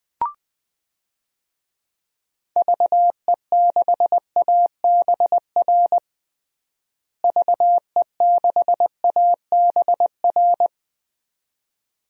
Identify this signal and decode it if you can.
Morse code, heard as a keyed tone, about 20 wpm, 700 Hz. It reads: VE6ABR VE6ABR
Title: DXCC-100 Call Signs - 6 Characters - Repeated 2x - 20wpm